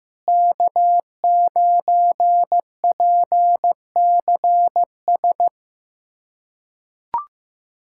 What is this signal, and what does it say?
Morse code, heard as a keyed tone, about 15 wpm, 700 Hz. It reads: K9PCS
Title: USA Callsign DE Callsign - 5 Characters - 15wpm